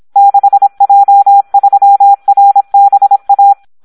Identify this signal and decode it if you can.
Morse code, heard as a keyed tone, about 26 wpm, 800 Hz. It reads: 6J3RBA
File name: Audio practica CW / SD 1 26